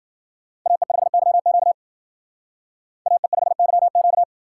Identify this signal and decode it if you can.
Morse code, heard as a keyed tone, about 47 wpm, 700 Hz. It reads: AE5YQ AE5YQ
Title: Speed Racing - USA 2x2 Call Signs 40wpm